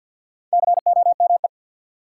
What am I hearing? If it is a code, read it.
Morse code, heard as a keyed tone, about 53 wpm, 700 Hz. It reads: COME
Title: Sets of 2 Words - New Word Drill - 50wpm